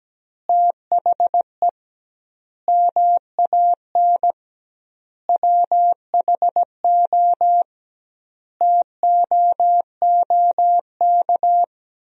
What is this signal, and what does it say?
Morse code, heard as a keyed tone, about 17 wpm, 700 Hz. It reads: THE MAN WHO TOOK
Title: 2x Word-Spacing - Sentences from Top 300 Words - 17wpm